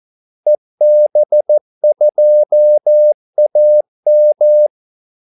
Morse code, heard as a keyed tone, about 14 wpm, 600 Hz. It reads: EB2AM
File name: Callsigns-14wpm